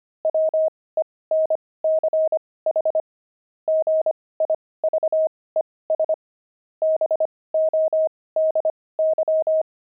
Morse code, heard as a keyed tone, about 25 wpm, spaced as for 18 wpm, 630 Hz. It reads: WENCH GIVES BODY
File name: pangram-fs-18-cs-25-jolt